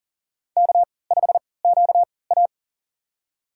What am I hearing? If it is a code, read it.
Morse code, heard as a keyed tone, about 40 wpm, spaced as for 23 wpm, 700 Hz. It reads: K5QA
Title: USA Call Signs - 4 Characters - 3x Character Spacing - 40wpm